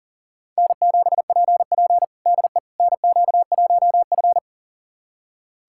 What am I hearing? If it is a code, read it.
Morse code, heard as a keyed tone, about 40 wpm, 700 Hz. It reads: N7PP DE NQ1F
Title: USA Callsign DE Callsign - 4 Characters - 40wpm